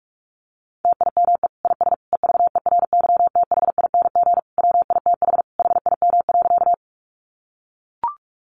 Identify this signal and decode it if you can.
Morse code, heard as a keyed tone, about 44 wpm, 700 Hz. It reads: TIME IS EVERYTHING WITH HIM.
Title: Sentences from Top 700 Words 45wpm